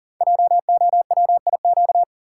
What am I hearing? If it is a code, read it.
Morse code, heard as a keyed tone, about 41 wpm, 690 Hz. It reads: JOWIQ
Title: Code Groups - 5 Letters - 40wpm